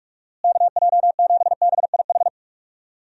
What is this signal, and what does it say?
Morse code, heard as a keyed tone, about 45 wpm, 700 Hz. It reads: KJ7BIH
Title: USA Call Signs - 6 Characters - Repeated 3x - 45wpm